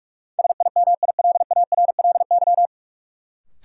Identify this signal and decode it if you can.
Morse code, heard as a keyed tone, about 46 wpm, 700 Hz. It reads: SIMILARLY
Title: Sentences from Top 1400 Words - New Word Drill - 45wpm